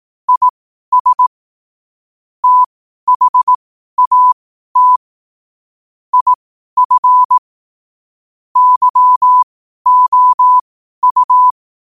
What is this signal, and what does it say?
Morse code, heard as a keyed tone, about 18 wpm, spaced as for 12 wpm, 1000 Hz. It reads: IS THAT IF YOU
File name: qotdfiles / 12wpm